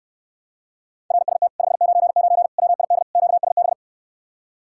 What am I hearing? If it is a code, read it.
Morse code, heard as a keyed tone, about 69 wpm, 700 Hz. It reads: RST 599 WID QSB